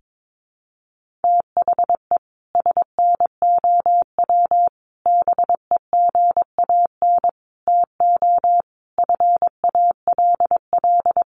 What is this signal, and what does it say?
Morse code, heard as a keyed tone, about 22 wpm, 700 Hz. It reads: THE SNOW BEGAN TO FALL
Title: Sentences from Top 1500 Words 22wpm